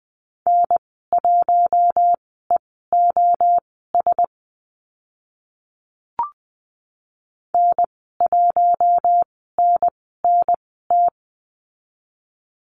Morse code, heard as a keyed tone, about 20 wpm, spaced as for 15 wpm, 700 Hz. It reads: N1EOS N1NNT
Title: 042 Lesson 8g - Number 1 - Call Signs 20wpm